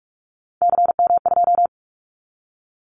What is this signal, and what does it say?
Morse code, heard as a keyed tone, about 44 wpm, 700 Hz. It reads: CM2
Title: Callsign Prefixes for Top 100 DXCC - 45wpm